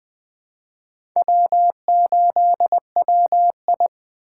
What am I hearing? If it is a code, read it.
Morse code, heard as a keyed tone, about 25 wpm, 700 Hz. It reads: W8WI
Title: Speed Racing - USA 1x2 Call Signs 20wpm